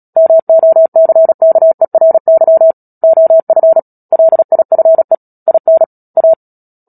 Morse code, heard as a keyed tone, about 37 wpm, 650 Hz. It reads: MOCKERY OF LIFE IN A
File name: Book Two / 10-Chapter 10 - Epilogue - 36 WPM